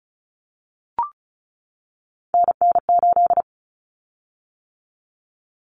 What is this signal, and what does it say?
Morse code, heard as a keyed tone, about 34 wpm, 700 Hz. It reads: NN8
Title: Common Cut Numbers - 3-Digit 35wpm